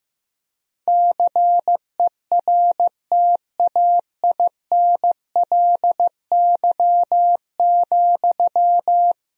Morse code, heard as a keyed tone, about 15 wpm, 700 Hz. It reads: CERTAINLY,
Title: Sentences from Top 1000 Words - Review Part 3 - 15wpm